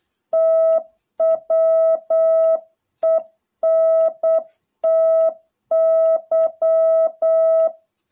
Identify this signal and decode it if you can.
Morse code, heard as a keyed tone, about 8 wpm, 660 Hz. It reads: TWENTY